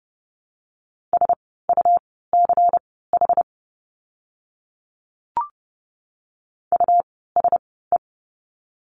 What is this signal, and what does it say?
Morse code, heard as a keyed tone, about 30 wpm, spaced as for 17 wpm, 700 Hz. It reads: SUCH USE
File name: course / 30:10wpm / 237 Lesson 38f - Letter q - Words 30wpm